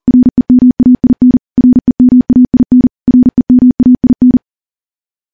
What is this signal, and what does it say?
Morse code, heard as a keyed tone, about 40 wpm, 250 Hz. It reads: REMAIN REMAIN REMAIN
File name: morse / morse-training-40WPM